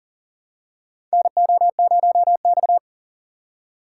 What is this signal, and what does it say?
Morse code, heard as a keyed tone, about 40 wpm, 700 Hz. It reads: NO0X
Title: USA Call Signs - 6x Character Spacing - 40wpm